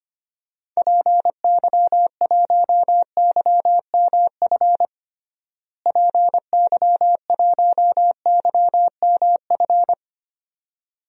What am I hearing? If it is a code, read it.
Morse code, heard as a keyed tone, about 25 wpm, 700 Hz. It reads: PY1YMF PY1YMF
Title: DXCC-100 Call Signs - 6 Characters - Repeated 2x - 25wpm